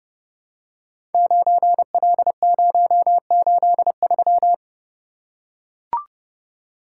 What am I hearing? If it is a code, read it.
Morse code, heard as a keyed tone, about 30 wpm, 700 Hz. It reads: 9L083
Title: Code Groups - 5 Letter:Numbers - 30wpm